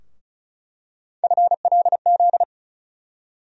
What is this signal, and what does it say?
Morse code, heard as a keyed tone, about 36 wpm, 700 Hz. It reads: FLZ
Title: USA Call Sign Postfixes - 3 Letters 35wpm